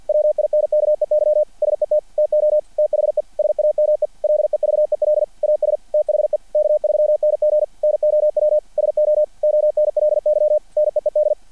Morse code, heard as a keyed tone, about 50 wpm, 600 Hz. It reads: WINKEY SET TO THE SAME LEVEL AS THE K3NG NOW SO ONLY NEED
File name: archived / CRSnet